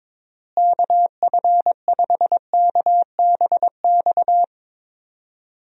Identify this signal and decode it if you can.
Morse code, heard as a keyed tone, about 22 wpm, 700 Hz. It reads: KF5KBX
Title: USA Call Signs - 6 Characters - Repeated 2x - 22wpm